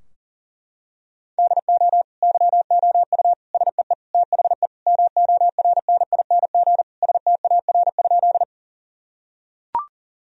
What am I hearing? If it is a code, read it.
Morse code, heard as a keyed tone, about 40 wpm, 700 Hz. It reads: DO YOU SEE THE MORNING STAR?
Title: Sentences from Top 1000 Words - Review Part 2 - 40wpm